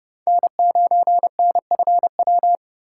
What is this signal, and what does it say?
Morse code, heard as a keyed tone, about 30 wpm, 700 Hz. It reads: N9NFW
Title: USA Call Signs - 5 Characters - 2x Character Spacing - 30wpm